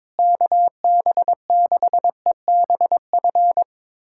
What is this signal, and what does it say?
Morse code, heard as a keyed tone, about 22 wpm, 700 Hz. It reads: KB6EBF
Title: USA Callsign DE Callsign 22wpm